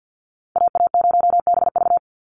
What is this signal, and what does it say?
Morse code, heard as a keyed tone, about 50 wpm, 700 Hz. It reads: AA0BV
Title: NA SOTA Activator Callsigns 50wpm